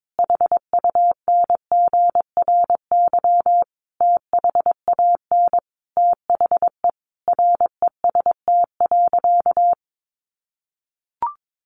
Morse code, heard as a keyed tone, about 22 wpm, 700 Hz. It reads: HUNGRY THAN THE REST.
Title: Sentences from Top 1400 Words 22wpm